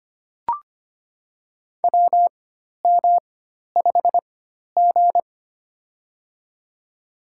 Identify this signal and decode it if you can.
Morse code, heard as a keyed tone, about 25 wpm, spaced as for 12 wpm, 700 Hz. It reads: WM5G
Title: USA Call Signs - 4 Characters - 4x Character Spacing - 25wpm